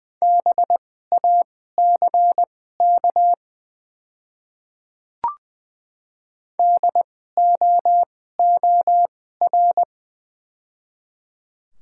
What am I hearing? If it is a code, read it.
Morse code, heard as a keyed tone, about 21 wpm, spaced as for 15 wpm, 690 Hz. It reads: BACK DOOR
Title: 2x Character Spacing - Top 500 Words - 20wpm